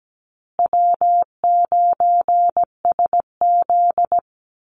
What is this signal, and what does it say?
Morse code, heard as a keyed tone, about 17 wpm, 700 Hz. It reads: W9SZ
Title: Call Signs Repeated 3x - 17wpm